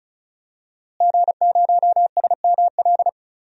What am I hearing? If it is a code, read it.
Morse code, heard as a keyed tone, about 35 wpm, 700 Hz. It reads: G0SML
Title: DXCC-100 Call Signs - 5 Characters - Repeated 2x - 35wpm